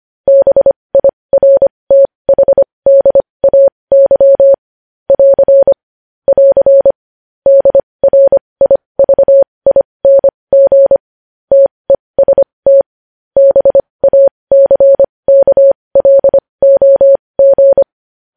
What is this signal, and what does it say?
Morse code, heard as a keyed tone, about 25 wpm, spaced as for 20 wpm, 550 Hz. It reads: BIRTHDAY <AR> <AR> DRIVING TEST BACKLOG
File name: podcast / News headlines-1x-25x20